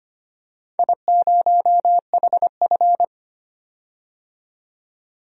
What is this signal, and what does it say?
Morse code, heard as a keyed tone, about 25 wpm, 700 Hz. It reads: I0HF
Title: DXCC-100 Call Signs - 4 Characters - 25wpm